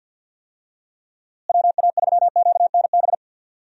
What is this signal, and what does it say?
Morse code, heard as a keyed tone, about 47 wpm, 700 Hz. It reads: WA3QNB
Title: USA Call Signs - 6 Characters - Repeated 2x - 50wpm